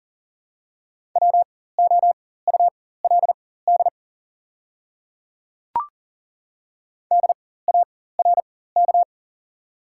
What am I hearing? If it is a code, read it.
Morse code, heard as a keyed tone, about 40 wpm, spaced as for 19 wpm, 690 Hz. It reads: WOULD DARK
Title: 4x Character Spacing - Top 500 Words - 40wpm